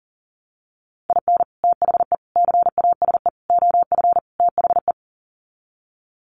Morse code, heard as a keyed tone, about 40 wpm, 700 Hz. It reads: IN THE CASE OF THE
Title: Sets of 5 Words 40wpm